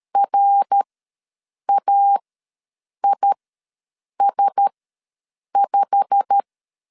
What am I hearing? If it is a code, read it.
Morse code, heard as a keyed tone, about 13 wpm, spaced as for 7 wpm, 800 Hz. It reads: RAIS5
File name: LEZIONI / 8 YJ45 / 8 YJ45 ws 12wpm 3146